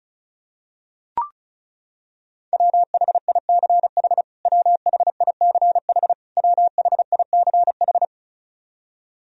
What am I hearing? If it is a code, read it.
Morse code, heard as a keyed tone, about 36 wpm, 700 Hz. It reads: WHICH WHICH WHICH
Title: Top 100 Words Repeated 35wpm